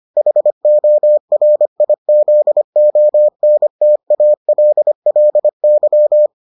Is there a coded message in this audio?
Yes: Morse code from 0.2 to 6.3 s, heard as a keyed tone, about 25 wpm, 600 Hz. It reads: HORIZONTALLY